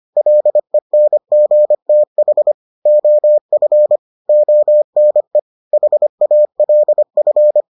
Morse code, heard as a keyed tone, about 25 wpm, 600 Hz. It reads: LENGTH OF ONE HALF